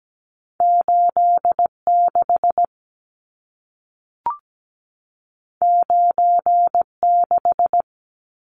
Morse code, heard as a keyed tone, about 17 wpm, 700 Hz. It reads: 86 96
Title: Numbers - 2-Digit 17wpm